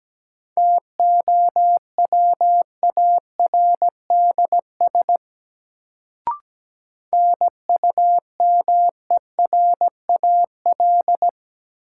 Morse code, heard as a keyed tone, about 17 wpm, 700 Hz. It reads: TOWARDS NUMERAL
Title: Sentences from Top 1000 Words - Review Part 2 - New Word Drill - 17wpm